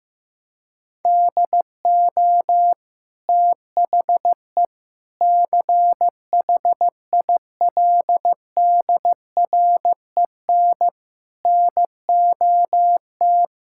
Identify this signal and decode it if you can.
Morse code, heard as a keyed tone, about 16 wpm, 700 Hz. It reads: DO THE CHILDREN NOT
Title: Sentences from Top 1000 Words - Review Part 6 - 15wpm